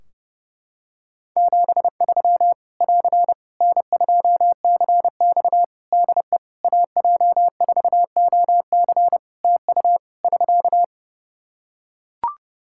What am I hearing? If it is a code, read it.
Morse code, heard as a keyed tone, about 30 wpm, 700 Hz. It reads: 73 <AR> N2CX DE AJ4OC TU <SK>
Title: Practice QSO - Exchange 4 - Ending - 30wpm